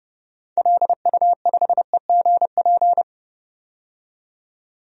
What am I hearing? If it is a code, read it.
Morse code, heard as a keyed tone, about 31 wpm, 700 Hz. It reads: LU5EGP
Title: DXCC-100 Call Signs - 6 Characters - 30wpm